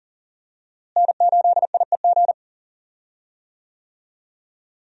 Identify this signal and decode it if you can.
Morse code, heard as a keyed tone, about 40 wpm, 700 Hz. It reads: N8IEG
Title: USA Call Signs - 5 Characters - Repeated 2x - 40wpm